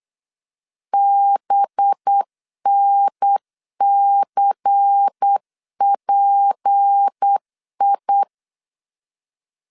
Morse code, heard as a keyed tone, about 8 wpm, 800 Hz. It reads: BNCPI
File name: LEZIONI / 6 PKX0 / 6 P 8wpm 3025